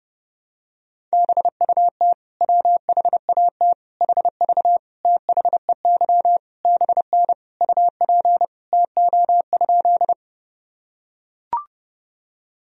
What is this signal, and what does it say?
Morse code, heard as a keyed tone, about 30 wpm, 700 Hz. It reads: BUT WHAT HV THEY BN UP TO?
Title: Sentences from Top 100 Words and Abbreviations 30wpm